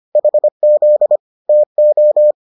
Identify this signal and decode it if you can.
Morse code, heard as a keyed tone, about 25 wpm, 600 Hz. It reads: HZ TO